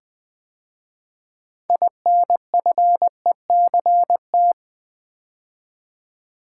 Morse code, heard as a keyed tone, about 20 wpm, 690 Hz. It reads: INFECT
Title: 6 Letter Words - First 3 Letters Given - 20wpm